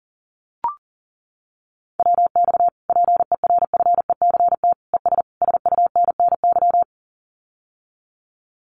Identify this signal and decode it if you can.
Morse code, heard as a keyed tone, about 40 wpm, 700 Hz. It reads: WX PERFECT ES SUNNY
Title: Common QSO Phrases Encore 40wpm